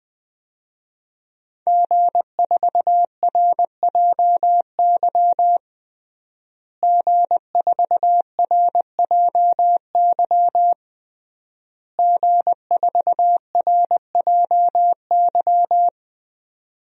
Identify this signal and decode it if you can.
Morse code, heard as a keyed tone, about 20 wpm, 700 Hz. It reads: G4RJY G4RJY G4RJY
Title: DXCC-100 Call Signs - 5 Characters - Repeated 3x - 20wpm